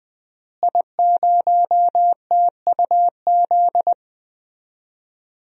Morse code, heard as a keyed tone, about 21 wpm, 700 Hz. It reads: I0TUZ
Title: DXCC-100 Call Signs - 5 Characters - Repeated 3x - 20wpm